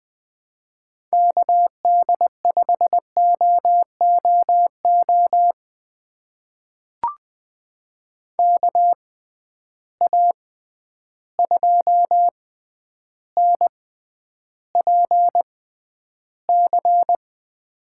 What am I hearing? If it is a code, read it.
Morse code, heard as a keyed tone, about 21 wpm, spaced as for 10 wpm, 700 Hz. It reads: KD5OOO KA2NPC
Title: USA Call Signs - 6 Characters - 6x Character Spacing - 20wpm